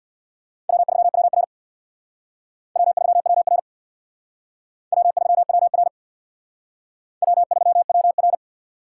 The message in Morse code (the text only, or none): W3WR W3WR W3WR W3WR